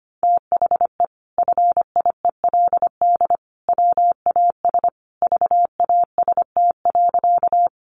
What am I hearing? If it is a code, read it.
Morse code, heard as a keyed tone, about 25 wpm, 700 Hz. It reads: THE FIELD WAS VAST.
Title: Sentences from Top 1300 Words 25wpm